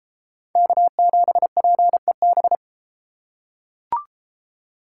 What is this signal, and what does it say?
Morse code, heard as a keyed tone, about 33 wpm, 700 Hz. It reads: K7PEB